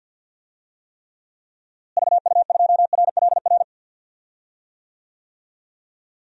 Morse code, heard as a keyed tone, about 51 wpm, 700 Hz. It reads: VU2RLR
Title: DXCC-100 Call Signs - 6 Characters - Repeated 3x - 50wpm